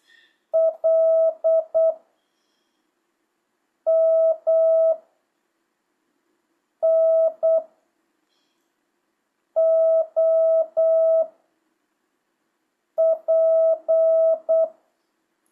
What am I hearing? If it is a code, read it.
Morse code, heard as a keyed tone, about 8 wpm, 660 Hz. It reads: L M N O P
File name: jh8cbh / 2025cw